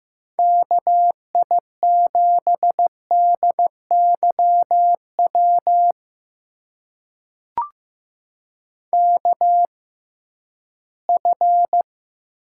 Morse code, heard as a keyed tone, about 15 wpm, spaced as for 11 wpm, 700 Hz. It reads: KI7DYW KF
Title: USA Call Signs - 6 Characters - 6x Character Spacing - 15wpm